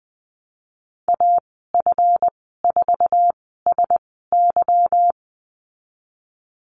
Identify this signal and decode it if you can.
Morse code, heard as a keyed tone, about 20 wpm, spaced as for 15 wpm, 700 Hz. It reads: AF4SY
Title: 193 Lesson 31g - Letter v - Call Signs 20wpm